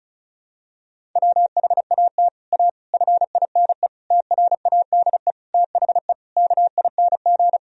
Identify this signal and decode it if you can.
Morse code, heard as a keyed tone, about 34 wpm, 700 Hz. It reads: WHAT A FINE TRADE THE KING
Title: Sentences from Top 1000 Words - Review Part 4 - 35wpm